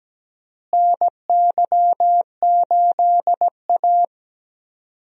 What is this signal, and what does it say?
Morse code, heard as a keyed tone, about 17 wpm, 700 Hz. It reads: NY8A
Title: USA Call Signs - 4 Characters - Repeated 3x - 17wpm